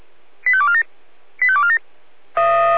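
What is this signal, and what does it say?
Five-tone selective call (ZVEI), heard as quick runs of five tones, tones of about 70 ms. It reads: 85247 85247
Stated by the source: ZVEI five-tone selective call received by SDR.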